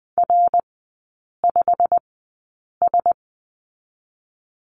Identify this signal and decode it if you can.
Morse code, heard as a keyed tone, about 20 wpm, spaced as for 8 wpm, 700 Hz. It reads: R5S
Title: 093 Lesson 16e - Letter c - Three Characters 20wpm